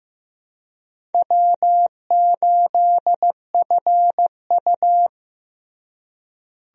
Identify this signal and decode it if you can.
Morse code, heard as a keyed tone, about 15 wpm, 700 Hz. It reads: W8FU
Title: USA Call Signs - 4 Characters - Repeated 2x - 15wpm